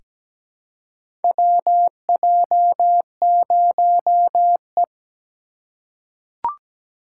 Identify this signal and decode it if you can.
Morse code, heard as a keyed tone, about 17 wpm, 700 Hz. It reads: WJ0E
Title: USA Call Signs - 4 Characters - 3x Character Spacing - 17wpm